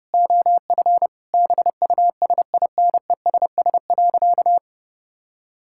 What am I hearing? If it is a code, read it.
Morse code, heard as a keyed tone, about 30 wpm, 700 Hz. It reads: OF BUSINESS.
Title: Sentences from Top 1000 Words - Review Part 5 - 30wpm